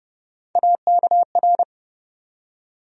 Morse code, heard as a keyed tone, about 32 wpm, 700 Hz. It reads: AKR
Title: USA Call Sign Postfixes - 3 Letters 30wpm